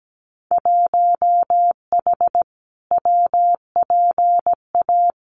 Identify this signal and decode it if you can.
Morse code, heard as a keyed tone, about 17 wpm, 700 Hz. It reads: 1H WPA
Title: Winter Field Day Exchanges 17wpm